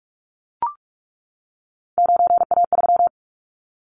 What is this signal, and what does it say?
Morse code, heard as a keyed tone, about 44 wpm, 700 Hz. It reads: 9A3
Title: Callsign Prefixes for Top 100 DXCC - 45wpm